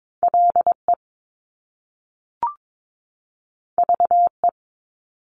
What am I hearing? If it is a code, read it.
Morse code, heard as a keyed tone, about 22 wpm, 700 Hz. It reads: LE VE
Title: Top Two Letter Combinations in Words 22wpm